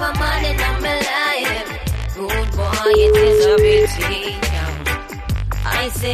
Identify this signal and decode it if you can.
Ringback tone (Europe, 425 Hz), with speech.